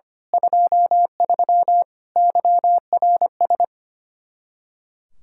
Morse code, heard as a keyed tone, about 26 wpm, 700 Hz. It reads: 23 YRS
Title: Speed-Racing Common QSO Phrases - 25wpm